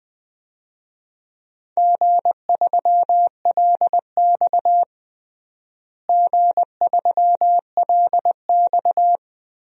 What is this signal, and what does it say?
Morse code, heard as a keyed tone, about 20 wpm, 700 Hz. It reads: G3LX G3LX
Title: DXCC-100 Call Signs - 4 Characters - Repeated 3x - 20wpm